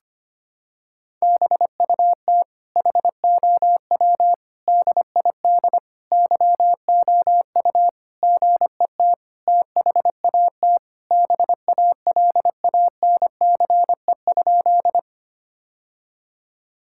Morse code, heard as a keyed tone, about 25 wpm, 700 Hz. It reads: BUT HOW DID YOU GET THAT BALANCE?
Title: Sentences from Top 1000 Words - Review Part 6 - 25wpm